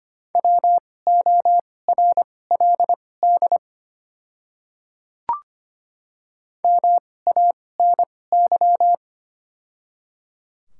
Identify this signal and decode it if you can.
Morse code, heard as a keyed tone, about 25 wpm, spaced as for 18 wpm, 700 Hz. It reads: WORLD MANY
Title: 2x Character Spacing - Top 200 Words - 25wpm